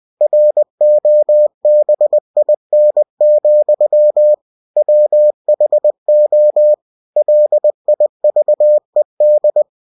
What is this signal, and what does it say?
Morse code, heard as a keyed tone, about 20 wpm, 600 Hz. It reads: ROBIN, WHO LIVED
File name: cw-books / winnie-the-pooh / winnie-20wpm-page-0004